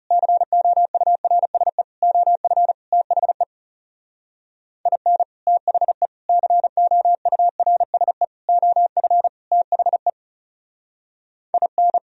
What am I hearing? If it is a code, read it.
Morse code, heard as a keyed tone, about 38 wpm, 700 Hz. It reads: COURSE OF THE IN THE COURSE OF THE IN